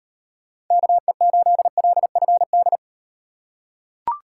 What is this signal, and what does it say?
Morse code, heard as a keyed tone, about 38 wpm, 700 Hz. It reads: KE8LFD